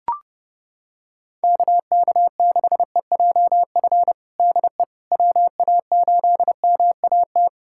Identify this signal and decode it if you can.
Morse code, heard as a keyed tone, about 32 wpm, 700 Hz. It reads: KK6EJF DE WA8MAT
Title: USA Callsign DE Callsign 30wpm